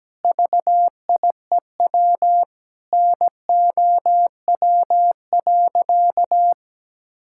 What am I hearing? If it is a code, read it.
Morse code, heard as a keyed tone, about 17 wpm, 700 Hz. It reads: VIEW NOW.
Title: Sentences from Top 1000 Words - Review Part 6 - 17wpm